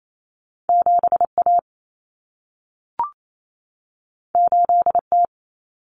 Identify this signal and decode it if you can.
Morse code, heard as a keyed tone, about 28 wpm, 700 Hz. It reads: 7A 8T
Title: Common Cut Numbers - 2-Digit 28wpm